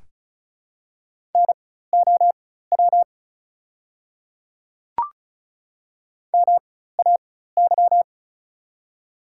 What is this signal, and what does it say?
Morse code, heard as a keyed tone, about 36 wpm, spaced as for 16 wpm, 700 Hz. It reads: NOW MAY
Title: 4x Character Spacing - Top 200 Words - 35wpm